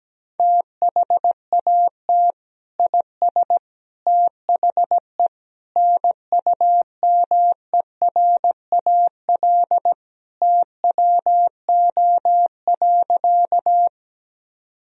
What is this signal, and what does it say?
Morse code, heard as a keyed tone, about 17 wpm, 700 Hz. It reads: THAT IS THE NUMERAL TWO.
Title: Sentences from Top 1000 Words - Review Part 2 - 17wpm